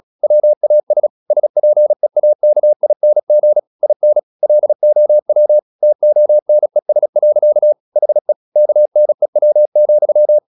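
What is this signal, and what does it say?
Morse code, heard as a keyed tone, about 36 wpm, 600 Hz. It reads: WAS SPEAKING IN LOW TONES. HE KNEW,
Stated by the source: Morse practice at 36 wpm, text of Ken Follett's novel Never, from CW med Gnister.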